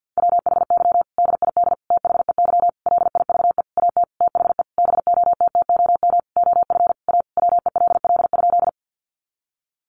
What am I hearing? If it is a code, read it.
Morse code, heard as a keyed tone, about 50 wpm, 700 Hz. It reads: WHY DID THEY LIVE AT THE BOTTOM OF A WELL?
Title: Sentences from Top 900 Words 50wpm